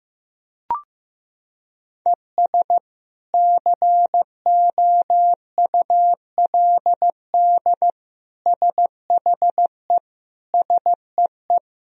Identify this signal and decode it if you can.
Morse code, heard as a keyed tone, about 15 wpm, 700 Hz. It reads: ES COULD SHE SEE
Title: Sentences from Top 100 Words and Abbreviations 15wpm